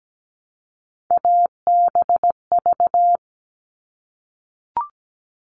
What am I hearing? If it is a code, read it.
Morse code, heard as a keyed tone, about 17 wpm, 700 Hz. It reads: ABV
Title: All Cut Numbers - 3-Digit 17wpm